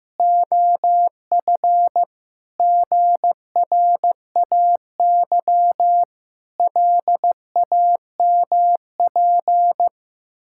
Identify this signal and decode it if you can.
Morse code, heard as a keyed tone, about 15 wpm, 700 Hz. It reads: OF GRAY LAMP
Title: Pangram Challenge v2 - 15wpm